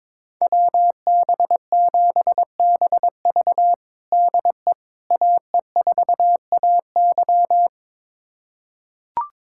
Morse code, heard as a keyed tone, about 22 wpm, 700 Hz. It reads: WB7BV DE AE4AY
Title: USA Callsign DE Callsign - 5 Characters - 22wpm